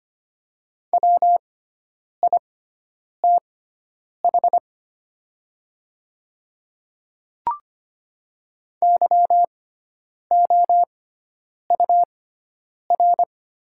Morse code, heard as a keyed tone, about 25 wpm, spaced as for 9 wpm, 700 Hz. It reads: WITH YOUR
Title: 6x Character Spacing - Top 100 Words - 25wpm